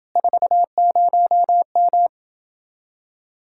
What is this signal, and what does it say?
Morse code, heard as a keyed tone, about 29 wpm, 700 Hz. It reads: 40M